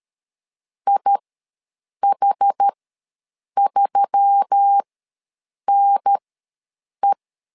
Morse code, heard as a keyed tone, about 13 wpm, spaced as for 7 wpm, 800 Hz. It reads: IH3NE
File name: LEZIONI / 4 Q1S3 ws 12wpm 2038